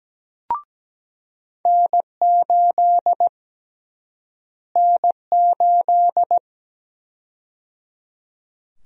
Morse code, heard as a keyed tone, about 17 wpm, 700 Hz. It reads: N8 N8
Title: USA Call Sign Prefixes - Repeated 2x - 17wpm